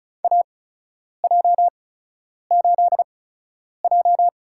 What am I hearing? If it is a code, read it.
Morse code, heard as a keyed tone, about 36 wpm, spaced as for 10 wpm, 700 Hz. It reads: AJ8J
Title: USA Call Signs - 4 Characters - 8x Character Spacing - 35wpm